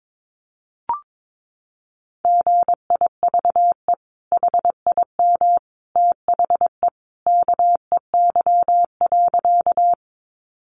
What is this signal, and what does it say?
Morse code, heard as a keyed tone, about 22 wpm, 700 Hz. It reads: GIVE HIM THE KEY.
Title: Sentences from Top 600 Words 22wpm